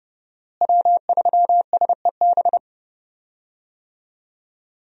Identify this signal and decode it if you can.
Morse code, heard as a keyed tone, about 30 wpm, 700 Hz. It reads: W3SEB
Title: USA Call Signs - 5 Characters - Repeated 2x - 30wpm